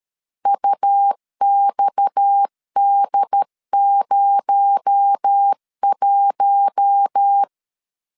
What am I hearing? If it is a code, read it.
Morse code, heard as a keyed tone, about 13 wpm, 800 Hz. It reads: UXD01
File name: CW / DET / 1 6 1500C S C 12wpm 3533